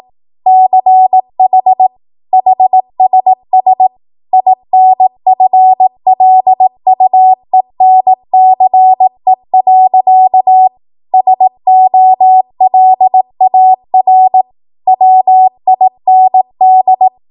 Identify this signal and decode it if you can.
Morse code, heard as a keyed tone, about 18 wpm, 750 Hz. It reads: CH HSS INFLUENCE. SOLAR WIND
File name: Morse / 18WPM